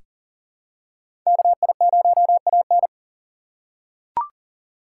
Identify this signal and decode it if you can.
Morse code, heard as a keyed tone, about 40 wpm, 700 Hz. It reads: KI0AN
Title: USA Call Signs - 5 Characters - 4x Character Spacing - 40wpm